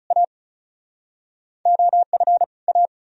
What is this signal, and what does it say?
Morse code, heard as a keyed tone, about 49 wpm, spaced as for 42 wpm, 700 Hz. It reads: A OF A
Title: Speed-Racing - Sets of 2 Words 35wpm